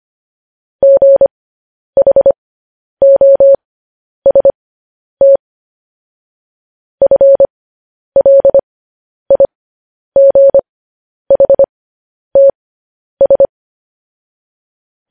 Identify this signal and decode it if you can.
Morse code, heard as a keyed tone, about 25 wpm, spaced as for 10 wpm, 550 Hz. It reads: GHOST FLIGHTS
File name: podcast / News headlines-1x-25x10